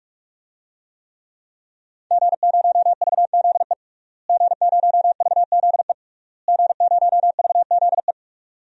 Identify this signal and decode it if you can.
Morse code, heard as a keyed tone, about 48 wpm, 700 Hz. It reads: G0VZE G0VZE G0VZE
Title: DXCC-100 Call Signs - 5 Characters - Repeated 3x - 45wpm